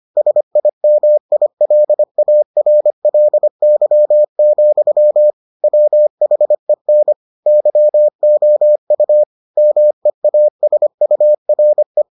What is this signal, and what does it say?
Morse code, heard as a keyed tone, about 25 wpm, 600 Hz. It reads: SIMILARLY, WHEN YOU MEASURE